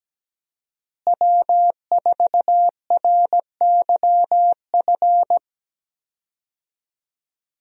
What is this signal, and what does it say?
Morse code, heard as a keyed tone, about 17 wpm, 700 Hz. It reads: W4RYF
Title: USA Call Signs - 5 Characters - Repeated 2x - 17wpm